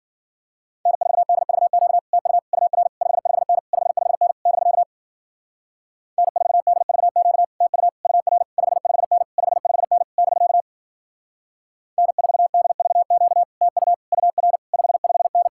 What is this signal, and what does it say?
Morse code, heard as a keyed tone, about 52 wpm, 700 Hz. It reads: N4DVQ TU UR 55N 55N <BK> N4DVQ TU UR 55N 55N <BK> N4DVQ TU UR 55N